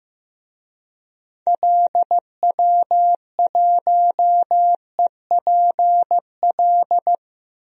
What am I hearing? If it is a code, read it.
Morse code, heard as a keyed tone, about 15 wpm, 700 Hz. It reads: LW1EPL
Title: DXCC-100 Call Signs - 6 Characters - 15wpm